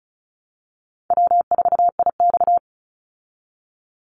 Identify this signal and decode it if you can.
Morse code, heard as a keyed tone, about 34 wpm, 700 Hz. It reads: W4IX
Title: Easy Call Signs Repeated 3x - 35wpm